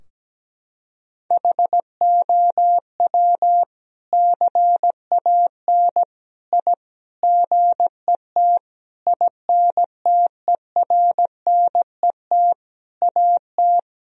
Morse code, heard as a keyed tone, about 17 wpm, 700 Hz. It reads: HOW CAN I GET INTERNET AT
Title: Sentences from Top 1000 Words - Review Part 4 - 17wpm